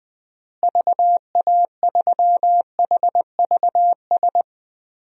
Morse code, heard as a keyed tone, about 20 wpm, 700 Hz. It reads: VA3HVS
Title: DXCC-100 Call Signs - 6 Characters - Repeated 2x - 20wpm